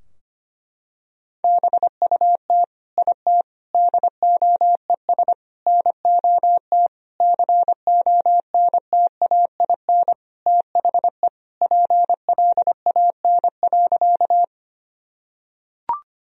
Morse code, heard as a keyed tone, about 25 wpm, 700 Hz. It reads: BUT IT DOES NOT CONTAIN THE PLAN.
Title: Sentences from Top 1000 Words - Review Part 2 - 25wpm